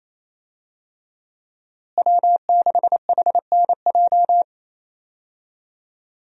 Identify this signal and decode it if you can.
Morse code, heard as a keyed tone, about 27 wpm, 700 Hz. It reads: W6HNJ
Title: USA Call Signs - 5 Characters - Repeated 3x - 28wpm